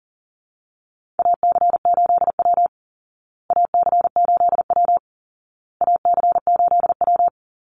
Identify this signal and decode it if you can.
Morse code, heard as a keyed tone, about 40 wpm, 700 Hz. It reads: AC8W AC8W AC8W
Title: Call Signs Repeated 2x - 40wpm